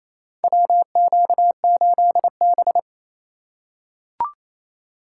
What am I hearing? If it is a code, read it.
Morse code, heard as a keyed tone, about 28 wpm, 700 Hz. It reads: WQ8B
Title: USA Call Signs - 4 Characters - 3x Character Spacing - 28wpm